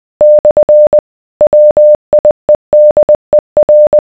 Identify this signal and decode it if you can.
Morse code, heard as a keyed tone, about 20 wpm, 600 Hz. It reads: / WIEDER